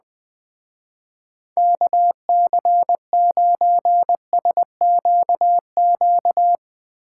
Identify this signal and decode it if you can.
Morse code, heard as a keyed tone, about 19 wpm, 700 Hz. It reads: KC9SQQ
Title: Speed Racing - USA 2x3 Call Signs 20wpm